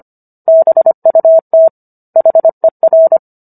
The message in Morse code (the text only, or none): BUT HER